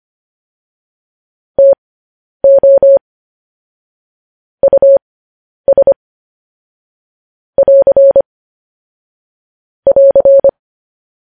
Morse code, heard as a keyed tone, about 25 wpm, spaced as for 10 wpm, 550 Hz. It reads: TO US <AR> <AR>
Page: news-headlines-in-morse-code-at-10-wpm